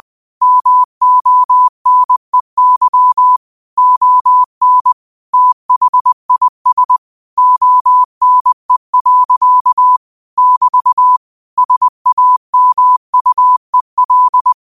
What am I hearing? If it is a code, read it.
Morse code, heard as a keyed tone, about 20 wpm, 1000 Hz. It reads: MONEY ON THIS ONE. <BT> SAMUEL
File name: qotdfiles / 20wpm